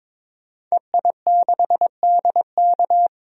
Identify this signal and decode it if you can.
Morse code, heard as a keyed tone, about 21 wpm, 700 Hz. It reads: EI6DK
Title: DXCC-100 Call Signs - 5 Characters - 22wpm